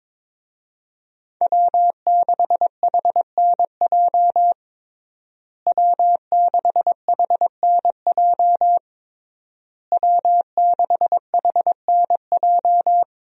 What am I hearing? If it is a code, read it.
Morse code, heard as a keyed tone, about 22 wpm, 700 Hz. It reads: W6HNJ W6HNJ W6HNJ
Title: USA Call Signs - 5 Characters - Repeated 3x - 22wpm